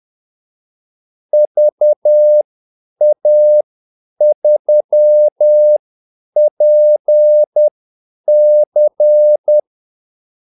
Morse code, heard as a keyed tone, about 10 wpm, spaced as for 8 wpm, 600 Hz. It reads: VA3PC
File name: Callsigns-10-8wpm